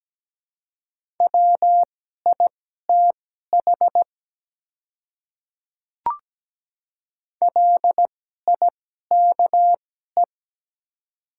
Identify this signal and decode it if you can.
Morse code, heard as a keyed tone, about 17 wpm, spaced as for 12 wpm, 690 Hz. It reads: WITH LIKE
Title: 2x Character Spacing - Top 100 Words - 17wpm